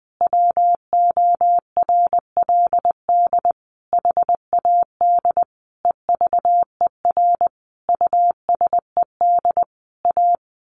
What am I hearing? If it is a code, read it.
Morse code, heard as a keyed tone, about 20 wpm, 700 Hz. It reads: WORLD HAD EVER USED A
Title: Sentences from Top 1100 Words 20wpm